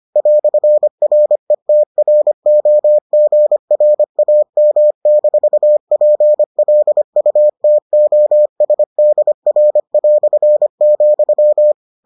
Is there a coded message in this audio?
Yes: Morse code from 0.2 to 11.7 s, heard as a keyed tone, about 25 wpm, 600 Hz. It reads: "RETROGRAM-PLUTOSDR",